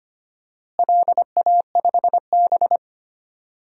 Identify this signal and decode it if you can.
Morse code, heard as a keyed tone, about 25 wpm, 700 Hz. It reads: LA5B
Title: DXCC-100 Call Signs - 4 Characters - Repeated 2x - 25wpm